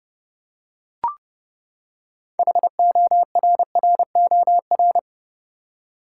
Morse code, heard as a keyed tone, about 31 wpm, 700 Hz. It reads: HORROR
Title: Six Letter Words - 30wpm